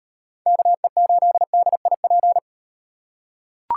Morse code, heard as a keyed tone, about 35 wpm, 700 Hz. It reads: KE8DIP